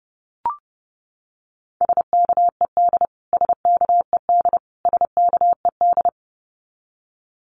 Morse code, heard as a keyed tone, about 30 wpm, 700 Hz. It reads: SKED SKED SKED
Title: Abbreviations Repeated 3x - 30wpm